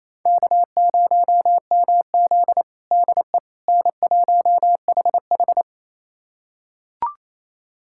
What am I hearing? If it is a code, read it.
Morse code, heard as a keyed tone, about 28 wpm, 700 Hz. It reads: K0MZ DE N1HH
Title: USA Callsign DE Callsign - 4 Characters - 28wpm